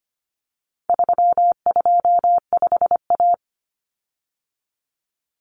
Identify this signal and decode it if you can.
Morse code, heard as a keyed tone, about 25 wpm, 700 Hz. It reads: 325A
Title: Common Cut Numbers - 4-Digit 25wpm